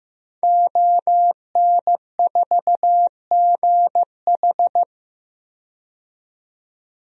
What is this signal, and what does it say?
Morse code, heard as a keyed tone, about 15 wpm, 700 Hz. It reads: ON4GH
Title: DXCC-100 Call Signs - 5 Characters - 15wpm